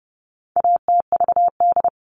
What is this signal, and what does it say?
Morse code, heard as a keyed tone, about 30 wpm, 700 Hz. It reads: ATVD
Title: All Cut Numbers - 4-Digit 30wpm